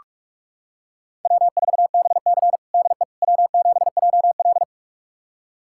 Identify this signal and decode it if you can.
Morse code, heard as a keyed tone, about 47 wpm, 700 Hz. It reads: W4BC DE W7JL
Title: USA Callsign DE Callsign - 4 Characters - 45wpm